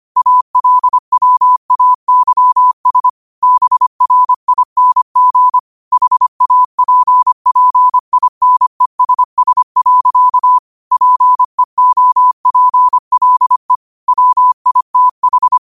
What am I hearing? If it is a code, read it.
Morse code, heard as a keyed tone, about 25 wpm, 1000 Hz. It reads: ALWAYS BRING HAPPINESS. PEOPLE WITH
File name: qotdfiles / 25wpm